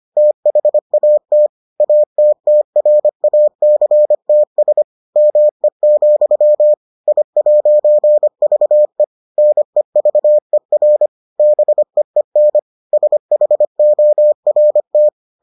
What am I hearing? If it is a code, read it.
Morse code, heard as a keyed tone, about 25 wpm, 600 Hz. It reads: THAT ATTRACTS ME, I'VE NEVER BEEN SHORT